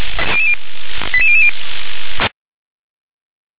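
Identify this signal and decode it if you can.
Five-tone selective call (ZVEI), heard as quick runs of five tones, tones of about 70 ms. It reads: 88000
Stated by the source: ZVEI five-tone call 88641 received by SDR.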